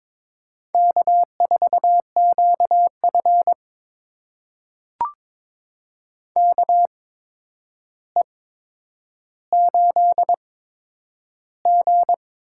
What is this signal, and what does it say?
Morse code, heard as a keyed tone, about 21 wpm, spaced as for 9 wpm, 700 Hz. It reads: K4QF KE8G
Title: USA Call Signs - 8x Character Spacing - 22wpm